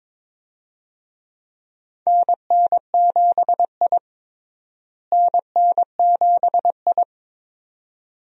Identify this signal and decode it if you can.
Morse code, heard as a keyed tone, about 23 wpm, 700 Hz. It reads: NN7I NN7I
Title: USA Call Signs - 4 Characters - Repeated 2x - 22wpm